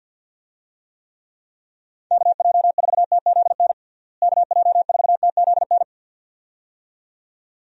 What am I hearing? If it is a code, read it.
Morse code, heard as a keyed tone, about 51 wpm, 700 Hz. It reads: KJ4TZN KJ4TZN
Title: USA Call Signs - 6 Characters - Repeated 2x - 50wpm